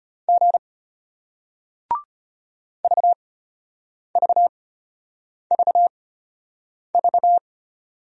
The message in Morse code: G VVVV

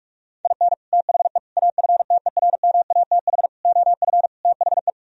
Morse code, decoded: IN THE AFTERMATH OF THE

